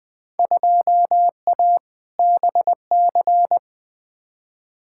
Morse code, heard as a keyed tone, about 20 wpm, 700 Hz. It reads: 2A BC
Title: Field Day Exchanges Encore - 20wpm